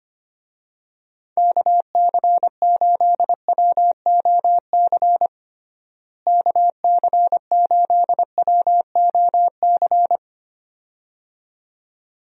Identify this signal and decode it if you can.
Morse code, heard as a keyed tone, about 25 wpm, 700 Hz. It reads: KC8WOC KC8WOC
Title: USA Call Signs - 6 Characters - Repeated 2x - 25wpm